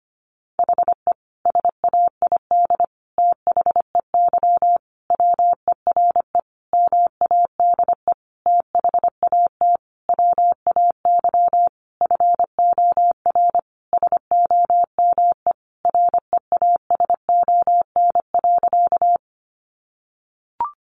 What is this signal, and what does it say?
Morse code, heard as a keyed tone, about 25 wpm, 700 Hz. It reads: HE SAID THEY WERE MADE THAT WAY FOR SOME REASON.
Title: Sentences from Top 700 Words 25wpm